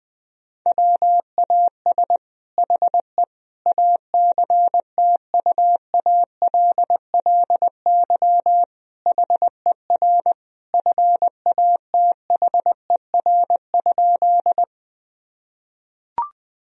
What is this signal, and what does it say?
Morse code, heard as a keyed tone, about 20 wpm, 700 Hz. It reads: WAS HE ACTUALLY HER FATHER?
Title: Sentences from Top 1000 Words - Review Part 4 - 20wpm